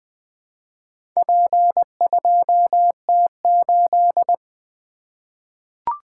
Morse code, heard as a keyed tone, about 20 wpm, 700 Hz. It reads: P2T8
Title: Four Letter-Number Combinations - 20wpm